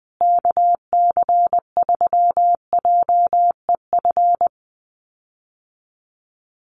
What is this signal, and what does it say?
Morse code, heard as a keyed tone, about 20 wpm, 700 Hz. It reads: KC3JEF
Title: USA Call Signs Repeated 3x 20wpm